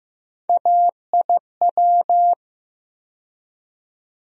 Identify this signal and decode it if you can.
Morse code, heard as a keyed tone, about 15 wpm, 700 Hz. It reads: AIW
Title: USA Call Sign Postfixes - 3 Letters 15wpm